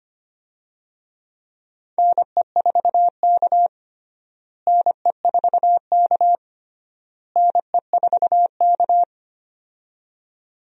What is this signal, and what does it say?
Morse code, heard as a keyed tone, about 25 wpm, 700 Hz. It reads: NE4K NE4K NE4K
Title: USA Call Signs - 4 Characters - Repeated 3x - 25wpm